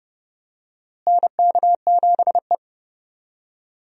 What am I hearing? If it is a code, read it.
Morse code, heard as a keyed tone, about 31 wpm, 700 Hz. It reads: NK7E
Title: USA Call Signs - 4 Characters - Repeated 2x - 30wpm